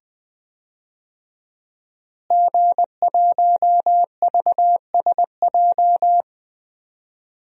Morse code, heard as a keyed tone, about 20 wpm, 700 Hz. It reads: G1VSJ
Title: DXCC-100 Call Signs - 5 Characters - Repeated 2x - 20wpm